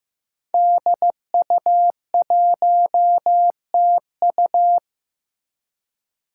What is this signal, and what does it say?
Morse code, heard as a keyed tone, about 15 wpm, 700 Hz. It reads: DU1TU
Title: Five Challenging Letter-Number Combinations 15wpm